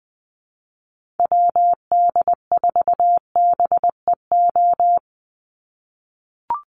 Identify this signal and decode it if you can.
Morse code, heard as a keyed tone, about 20 wpm, 700 Hz. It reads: WD4BEO
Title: USA Call Signs 20wpm